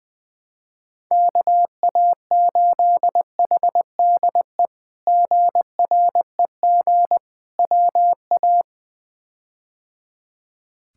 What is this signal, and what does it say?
Morse code, heard as a keyed tone, about 20 wpm, 700 Hz. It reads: KA8HDE GREG WA
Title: K1USN SST QSO Practice - 20wpm